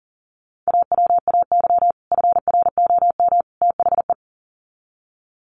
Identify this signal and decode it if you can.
Morse code, heard as a keyed tone, about 40 wpm, 700 Hz. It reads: AWAY FROM THE
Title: Sets of 3 Words 40wpm